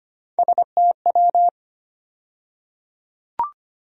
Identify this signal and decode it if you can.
Morse code, heard as a keyed tone, about 25 wpm, 700 Hz. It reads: STW
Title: USA Call Sign Postfixes - 3 Letters 25wpm